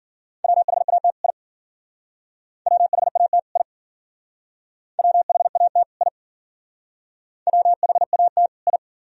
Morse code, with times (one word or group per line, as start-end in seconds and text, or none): 0.4-1.1 WHAT
1.2-1.3 I
2.7-3.4 WHAT
3.6-3.6 I
5.0-5.8 WHAT
6.0-6.1 I
7.5-8.5 WHAT
8.7-8.8 I